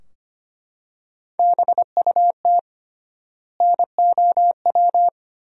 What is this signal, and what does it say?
Morse code, heard as a keyed tone, about 25 wpm, 700 Hz. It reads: BUT NOW
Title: 3x Word-Spacing - Sentences from Top 300 Words - 25wpm